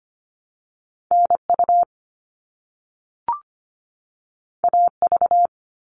Morse code, heard as a keyed tone, about 25 wpm, 700 Hz. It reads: NU AV
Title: All Cut Numbers - 2-Digit 25wpm